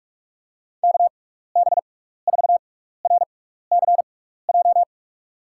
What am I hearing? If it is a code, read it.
Morse code, heard as a keyed tone, about 46 wpm, spaced as for 16 wpm, 700 Hz. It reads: KB4RCJ
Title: USA Call Signs - 6 Characters - 6x Character Spacing - 45wpm